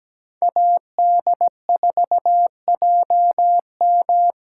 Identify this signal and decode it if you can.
Morse code, heard as a keyed tone, about 21 wpm, 700 Hz. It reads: AD4JM
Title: Speed Racing - USA 2x2 Call Signs 17wpm